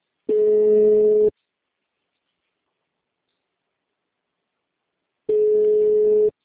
Ringback tone (Europe, 425 Hz).